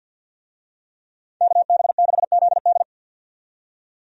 Morse code, heard as a keyed tone, about 52 wpm, 700 Hz. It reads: KB6ZD
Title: USA Call Signs - 5 Characters - 50wpm